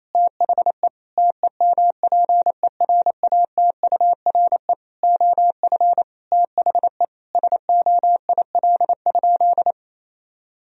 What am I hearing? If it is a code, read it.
Morse code, heard as a keyed tone, about 28 wpm, 700 Hz. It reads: THE TEMPERATURE OF THE SOIL?
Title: Sentences from Top 1000 Words - Review Part 4 - 28wpm